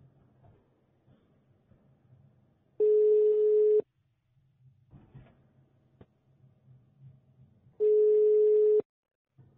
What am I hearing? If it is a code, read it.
Ringback tone (Europe, 425 Hz).